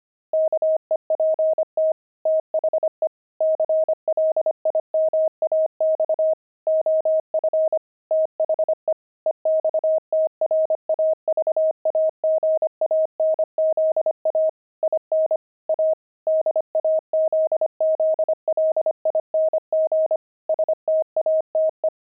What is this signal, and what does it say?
Morse code, heard as a keyed tone, about 25 wpm, 630 Hz. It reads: KEPT THE CLIMAX OF THE EXTRAVAGANZA IN A DAZZLING STATE
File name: pangram-fs-25-cs-25-jelly